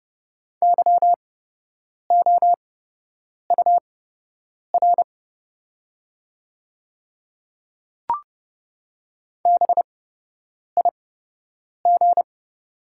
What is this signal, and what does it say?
Morse code, heard as a keyed tone, about 30 wpm, spaced as for 8 wpm, 700 Hz. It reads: YOUR BIG
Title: 8x Character Spacing - Top 300 Words - 30wpm